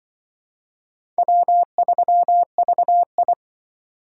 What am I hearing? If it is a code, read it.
Morse code, heard as a keyed tone, about 25 wpm, 700 Hz. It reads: W3VI
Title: Speed Racing - USA Call Signs 20wpm